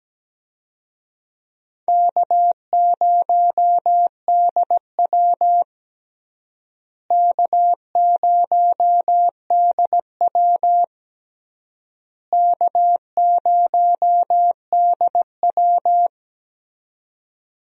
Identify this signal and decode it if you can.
Morse code, heard as a keyed tone, about 17 wpm, 700 Hz. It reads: K0DW K0DW K0DW
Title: USA Call Signs - 4 Characters - Repeated 3x - 17wpm